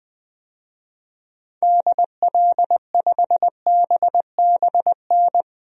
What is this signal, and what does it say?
Morse code, heard as a keyed tone, about 20 wpm, 700 Hz. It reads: DL5BBN
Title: DXCC-100 Call Signs - 6 Characters - 20wpm